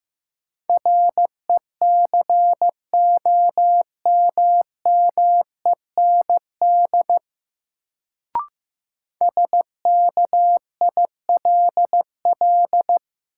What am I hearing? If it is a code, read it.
Morse code, heard as a keyed tone, about 15 wpm, 700 Hz. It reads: RECOMMEND SKILL
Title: Sentences from Top 1200 Words - New Word Drill - 15wpm